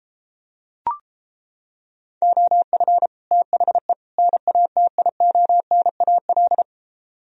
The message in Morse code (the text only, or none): OF THE NATIONAL